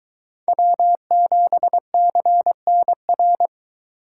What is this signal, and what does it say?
Morse code, heard as a keyed tone, about 22 wpm, 700 Hz. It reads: W7CNR